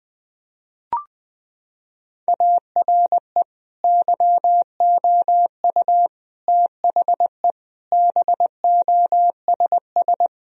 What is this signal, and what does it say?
Morse code, heard as a keyed tone, about 21 wpm, 700 Hz. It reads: ARE YOU THE BOSS
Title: Sentences from Top 1000 Words - Review Part 4 - 20wpm